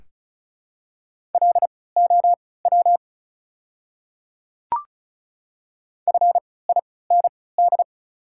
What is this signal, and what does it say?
Morse code, heard as a keyed tone, about 36 wpm, spaced as for 20 wpm, 700 Hz. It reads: LOW FIND
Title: 3x Character Spacing - Top 200 Words - 35wpm